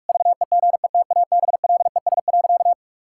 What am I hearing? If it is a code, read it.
Morse code, heard as a keyed tone, about 46 wpm, 700 Hz. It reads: VEGETABLES.
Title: Pangram Challenge v2 - 45wpm